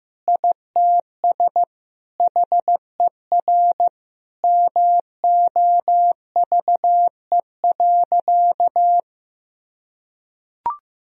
Morse code, heard as a keyed tone, about 15 wpm, 700 Hz. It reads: ITS HER MOVE.
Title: Sentences from Top 1000 Words - Review Part 1 - 15wpm